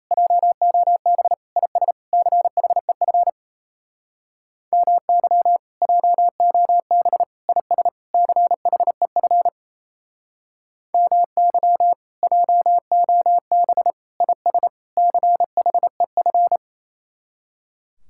Morse code, heard as a keyed tone, about 36 wpm, 700 Hz. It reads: JOB IS CHEF MY JOB IS CHEF MY JOB IS CHEF